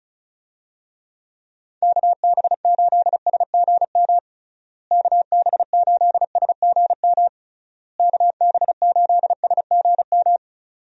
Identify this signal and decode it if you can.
Morse code, heard as a keyed tone, about 34 wpm, 700 Hz. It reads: KB8SGM KB8SGM KB8SGM
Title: USA Call Signs - 6 Characters - Repeated 3x - 35wpm